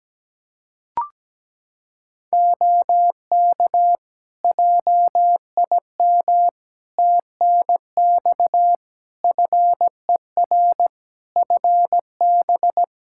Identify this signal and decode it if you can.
Morse code, heard as a keyed tone, about 17 wpm, 700 Hz. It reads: OK JIM TNX FER FB
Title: Practice QSO - Exchange 4 - Ending - 17wpm